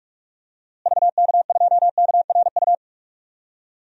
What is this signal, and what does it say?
Morse code, heard as a keyed tone, about 48 wpm, 700 Hz. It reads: VK2KRU
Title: DXCC-100 Call Signs - 6 Characters - Repeated 2x - 45wpm